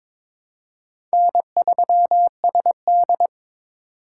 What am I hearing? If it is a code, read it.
Morse code, heard as a keyed tone, about 22 wpm, 700 Hz. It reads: N3SD
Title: USA Call Signs - 3x Character Spacing - 22wpm